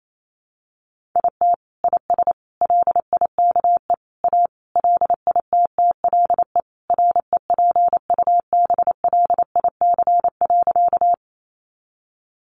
Morse code, heard as a keyed tone, about 28 wpm, 700 Hz. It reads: IT IS LIKE A LITTLE REPUBLIC.
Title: Sentences from Top 1500 Words 28wpm